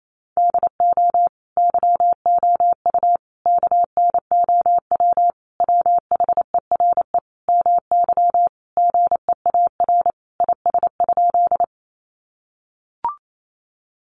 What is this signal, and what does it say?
Morse code, heard as a keyed tone, about 28 wpm, 700 Hz. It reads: DO YOU KNOW WHERE MY GEAR IS?
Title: Sentences from Top 1400 Words 28wpm